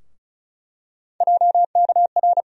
Morse code, heard as a keyed tone, about 34 wpm, 700 Hz. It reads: JKR
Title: USA Call Sign Postfixes - 3 Letters 35wpm